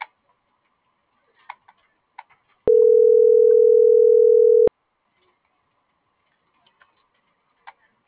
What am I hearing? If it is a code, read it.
Ringback tone (North America), 440 + 480 Hz.